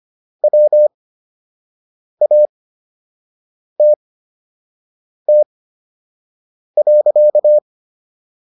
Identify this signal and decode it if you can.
Morse code, heard as a keyed tone, about 25 wpm, spaced as for 6 wpm, 600 Hz. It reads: WATT.